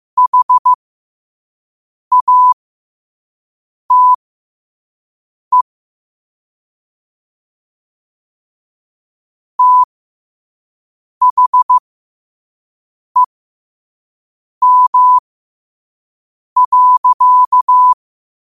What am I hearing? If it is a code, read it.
Morse code, heard as a keyed tone, about 15 wpm, spaced as for 5 wpm, 1000 Hz. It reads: HATE THEM.
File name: qotdfiles / 5wpm